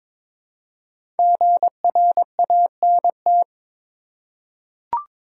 Morse code, heard as a keyed tone, about 22 wpm, 700 Hz. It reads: GRANT
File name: 5 Letter Words - First 3 Letters Given - 22wpm